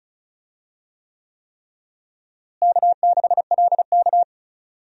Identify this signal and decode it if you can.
Morse code, heard as a keyed tone, about 34 wpm, 700 Hz. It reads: K6LK
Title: USA Call Signs - 4 Characters - Repeated 3x - 35wpm